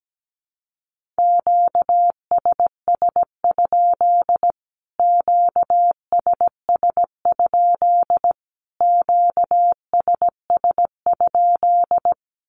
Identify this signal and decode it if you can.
Morse code, heard as a keyed tone, about 17 wpm, 700 Hz. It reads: QSS? QSS? QSS?
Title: Q-Codes Repeated 3x 17wpm